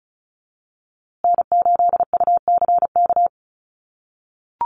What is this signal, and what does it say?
Morse code, heard as a keyed tone, about 34 wpm, 700 Hz. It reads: N8UCK
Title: USA Call Signs Repeated 2x 35wpm